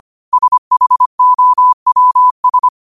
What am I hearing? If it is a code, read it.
Morse code, heard as a keyed tone, about 25 wpm, 1000 Hz. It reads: SHOWS
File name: qotdfiles / 25wpm